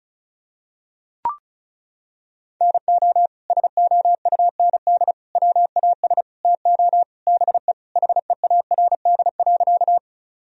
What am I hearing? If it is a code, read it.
Morse code, heard as a keyed tone, about 36 wpm, 690 Hz. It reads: NO SOUND WAS TO BE HEARD.
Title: Sentences from Top 1000 Words - Review Part 1 - 35wpm